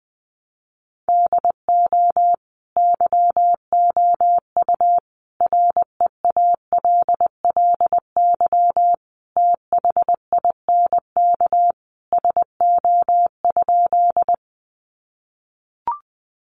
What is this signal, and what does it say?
Morse code, heard as a keyed tone, about 20 wpm, 700 Hz. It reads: DO YOU REALLY THINK SO?
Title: Sentences from Top 600 Words 20wpm